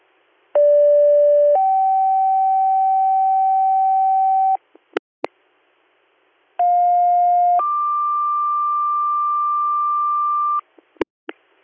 Two-tone page, heard as a one-second tone, then a three-second tone, about 580 then 770 Hz, about 710 then 1180 Hz. It is heard from 0.6 to 10.6 s.